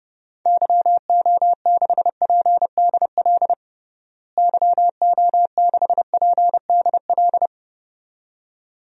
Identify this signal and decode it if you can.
Morse code, heard as a keyed tone, about 32 wpm, 700 Hz. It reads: YO6PDL YO6PDL
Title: DXCC-100 Call Signs - 6 Characters - Repeated 2x - 30wpm